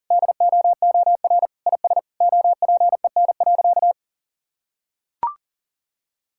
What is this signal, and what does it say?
Morse code, heard as a keyed tone, about 37 wpm, 700 Hz. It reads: DOOR IS OPEN.
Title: Sentences from Top 1000 Words - Review Part 6 - 40wpm